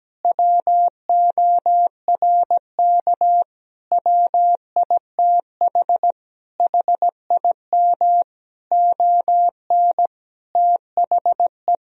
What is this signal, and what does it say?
Morse code, heard as a keyed tone, about 17 wpm, 700 Hz. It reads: WORK WITH HIM ON THE
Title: Sentences from Top 1000 Words - Review Part 4 - 17wpm